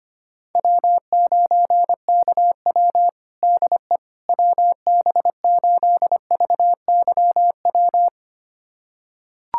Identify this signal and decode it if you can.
Morse code, heard as a keyed tone, about 25 wpm, 700 Hz. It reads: W9KW DE WB8VYW
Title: USA Callsign DE Callsign 25wpm